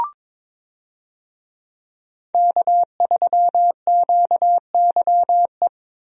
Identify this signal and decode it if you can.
Morse code, heard as a keyed tone, about 21 wpm, 700 Hz. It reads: K3QYE
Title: USA Call Signs - 5 Characters - Repeated 3x - 22wpm